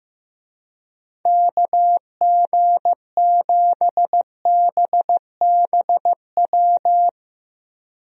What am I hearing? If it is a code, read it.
Morse code, heard as a keyed tone, about 15 wpm, 700 Hz. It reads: KG7BBW
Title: USA Call Signs - 6 Characters - Repeated 3x - 15wpm